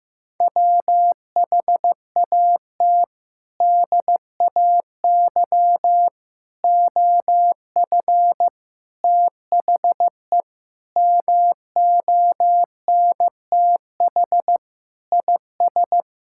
Morse code, heard as a keyed tone, about 15 wpm, 700 Hz. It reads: WHAT DAY OF THE MONTH IS
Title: Sentences from Top 1000 Words - Review Part 5 - 15wpm